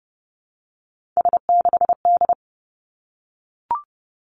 Morse code, heard as a keyed tone, about 30 wpm, 700 Hz. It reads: S6D
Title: Three Challenging Letter-Number Combinations 30wpm